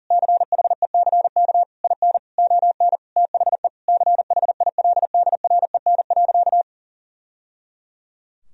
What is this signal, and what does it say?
Morse code, heard as a keyed tone, about 40 wpm, 700 Hz. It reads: CHECK IN ON THE CHILDREN.
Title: Sentences from Top 1000 Words - Review Part 3 - 40wpm